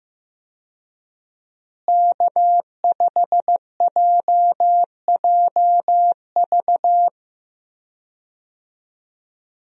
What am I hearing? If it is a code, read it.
Morse code, heard as a keyed tone, about 15 wpm, 700 Hz. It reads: K5JJV
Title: USA Call Signs - 5 Characters - Repeated 2x - 15wpm